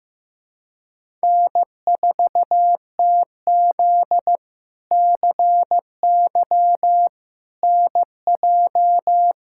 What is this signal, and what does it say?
Morse code, heard as a keyed tone, about 15 wpm, 700 Hz. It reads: N4TZ CY NJ
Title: K1USN SST QSO Practice - 15wpm